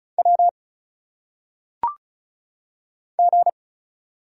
Morse code, heard as a keyed tone, about 36 wpm, 700 Hz. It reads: W G
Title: Single Letters - Phonetic Pronunciation 35wpm